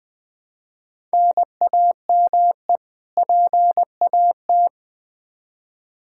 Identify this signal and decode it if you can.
Morse code, heard as a keyed tone, about 20 wpm, 700 Hz. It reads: NAME PAT
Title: Practice QSO - Exchange 1 - Beginning - 20wpm